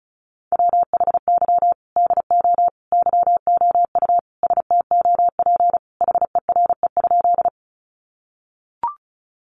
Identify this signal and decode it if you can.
Morse code, heard as a keyed tone, about 34 wpm, 700 Hz. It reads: WHY DO YOU STOP HERE?
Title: Sentences from Top 300 Words 35wpm